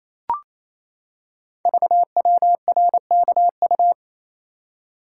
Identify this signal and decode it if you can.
Morse code, heard as a keyed tone, about 27 wpm, 700 Hz. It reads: VWRKU
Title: Code Groups - 5 Letter:Numbers - 28wpm